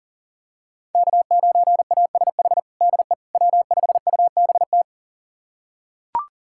Morse code, heard as a keyed tone, about 37 wpm, 700 Hz. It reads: K9ASH DE W5UBT
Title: USA Callsign DE Callsign - 5 Characters - 40wpm